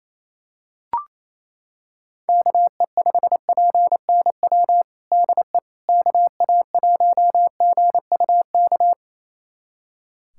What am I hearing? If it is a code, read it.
Morse code, heard as a keyed tone, about 28 wpm, 700 Hz. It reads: KE5PNW DE KA1GUK
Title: USA Callsign DE Callsign - 6 Characters - 28wpm